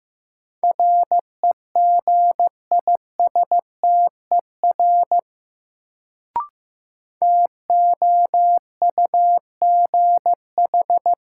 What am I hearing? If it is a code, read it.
Morse code, heard as a keyed tone, about 15 wpm, 700 Hz. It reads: REGISTER TOUGH
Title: Sentences from Top 1200 Words - New Word Drill - 15wpm